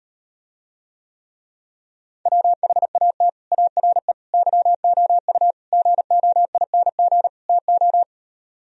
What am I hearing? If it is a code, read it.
Morse code, heard as a keyed tone, about 38 wpm, 700 Hz. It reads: WHAT ARE YOU GOING TO